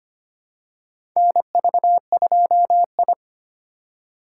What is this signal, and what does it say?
Morse code, heard as a keyed tone, about 25 wpm, 700 Hz. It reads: NV2I
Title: USA Call Signs - 4x Character Spacing - 25wpm